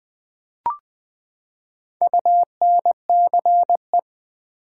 Morse code, heard as a keyed tone, about 20 wpm, 700 Hz. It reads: UNCE
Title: Build A Word - 4 Letters At A Time - 11-Letter Words - 20wpm